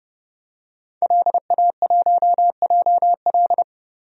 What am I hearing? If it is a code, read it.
Morse code, heard as a keyed tone, about 31 wpm, 700 Hz. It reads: LA1JL
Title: DXCC-100 Call Signs - 5 Characters - Repeated 2x - 30wpm